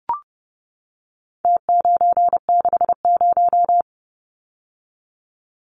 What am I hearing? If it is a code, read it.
Morse code, heard as a keyed tone, about 30 wpm, 700 Hz. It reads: T960
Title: Numbers - Contest Style - 4-Digit 30wpm